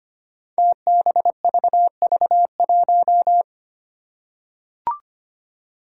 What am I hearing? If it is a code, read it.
Morse code, heard as a keyed tone, about 25 wpm, 700 Hz. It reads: TBVV1
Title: Five Challenging Letter-Number Combinations 25wpm